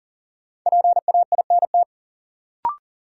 Morse code, heard as a keyed tone, about 40 wpm, 700 Hz. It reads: PAINT
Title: Sentences from Top 1000 Words - Review Part 6 - New Word Drill - 40wpm